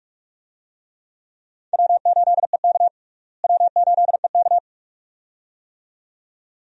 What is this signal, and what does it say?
Morse code, heard as a keyed tone, about 46 wpm, 700 Hz. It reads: W8EK W8EK
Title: USA Call Signs - 4 Characters - Repeated 2x - 45wpm